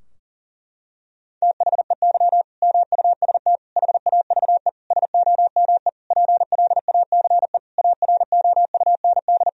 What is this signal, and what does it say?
Morse code, heard as a keyed tone, about 40 wpm, 700 Hz. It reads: THEY MUST HAVE SOME PLACE AROUND
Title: Sentences from Top 1000 Words - Review Part 3 - 40wpm